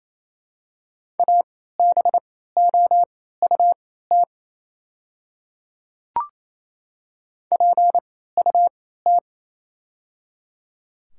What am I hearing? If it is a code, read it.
Morse code, heard as a keyed tone, about 28 wpm, spaced as for 16 wpm, 700 Hz. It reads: ABOUT PUT
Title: 3x Character Spacing - Top 200 Words - 28wpm